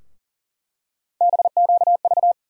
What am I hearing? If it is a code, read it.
Morse code, heard as a keyed tone, about 38 wpm, 700 Hz. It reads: BQV
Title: USA Call Sign Postfixes - 3 Letters 40wpm